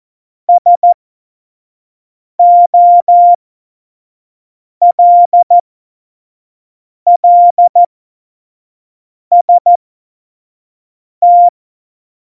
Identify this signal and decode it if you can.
Morse code, heard as a keyed tone, about 14 wpm, spaced as for 5 wpm, 700 Hz. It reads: SOLLST
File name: Morsekurs / Witze / Witz 14. August 2018, 25 F70